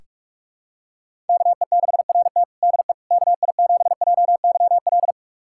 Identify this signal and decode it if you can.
Morse code, heard as a keyed tone, about 47 wpm, 700 Hz. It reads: KE6RT DE KI7JYL
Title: USA Callsign DE Callsign 45wpm